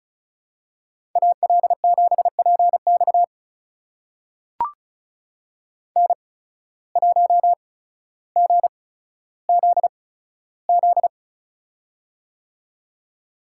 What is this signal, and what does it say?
Morse code, heard as a keyed tone, about 36 wpm, spaced as for 15 wpm, 700 Hz. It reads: AL7PX N1GZZ
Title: USA Call Signs - 5 Characters - 8x Character Spacing - 35wpm